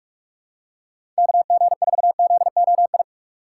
Morse code, heard as a keyed tone, about 44 wpm, 700 Hz. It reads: KG4ZOI
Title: USA Call Signs - 6 Characters - 3x Character Spacing - 45wpm